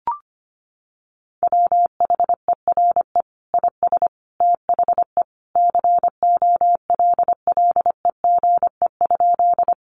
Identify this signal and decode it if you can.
Morse code, heard as a keyed tone, about 25 wpm, 700 Hz. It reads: WHERE IS THE COLLEGE?
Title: Sentences from Top 1300 Words 25wpm